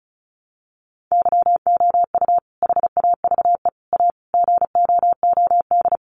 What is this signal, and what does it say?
Morse code, heard as a keyed tone, about 34 wpm, 700 Hz. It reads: YOU HAVE A GOOD
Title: Sentences from Top 1300 Words 35wpm